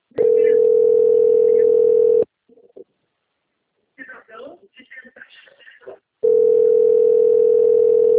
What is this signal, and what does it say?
Ringback tone (North America), 440 + 480 Hz, with speech.
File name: rg-1600-unknown- - - .8559